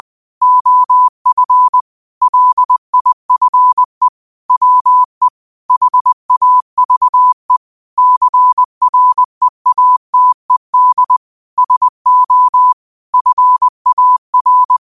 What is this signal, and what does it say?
Morse code, heard as a keyed tone, about 20 wpm, 1000 Hz. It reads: OF LIFE WE HAVE CREATED SO FAR
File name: qotdfiles / 20wpm